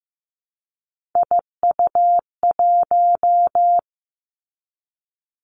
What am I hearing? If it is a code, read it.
Morse code, heard as a keyed tone, about 15 wpm, 700 Hz. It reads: IU1
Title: Three Challenging Letter-Number Combinations 15wpm